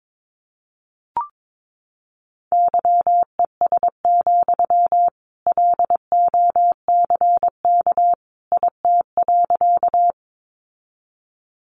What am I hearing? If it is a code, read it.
Morse code, heard as a keyed tone, about 22 wpm, 700 Hz. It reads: YES, LOCK IT.
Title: Sentences from Top 1300 Words 22wpm